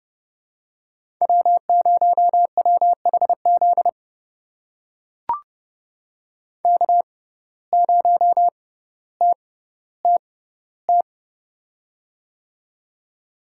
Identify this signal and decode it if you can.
Morse code, heard as a keyed tone, about 30 wpm, spaced as for 15 wpm, 700 Hz. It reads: W0WHZ K0TTT
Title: USA Call Signs - 5 Characters - 6x Character Spacing - 30wpm